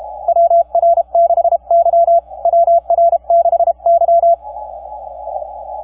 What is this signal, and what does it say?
Morse code, heard as a keyed tone, about 31 wpm, 680 Hz. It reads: WR6Y WR6Y
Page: the 40 Meter simplex CW pileup for KP5/NP3VI was so, sooo aggravating and beyond frustrating